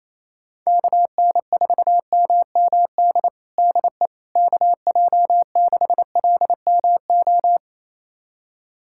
Morse code, heard as a keyed tone, about 27 wpm, 700 Hz. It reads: KN4MMD DE KJ6LMO
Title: USA Callsign DE Callsign - 6 Characters - 28wpm